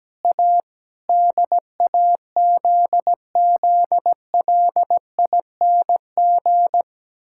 Morse code, heard as a keyed tone, about 17 wpm, 700 Hz. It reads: A DAZZLING
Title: Pangram Challenge v2 - 17wpm